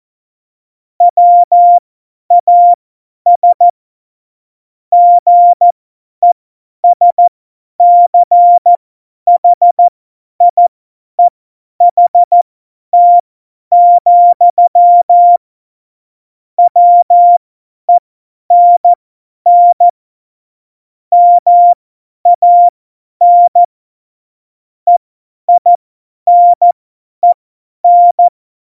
Morse code, heard as a keyed tone, about 14 wpm, spaced as for 10 wpm, 700 Hz. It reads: WAS GESCHIEHT, WENN MAN EINEN
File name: Morsekurs / Witze / Witz 08. Oktober 2019, 50 F70